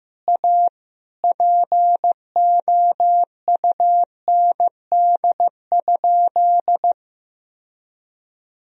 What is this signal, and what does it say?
Morse code, heard as a keyed tone, about 15 wpm, 700 Hz. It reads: A POUND?
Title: Sentences from Top 1000 Words - Review Part 2 - 15wpm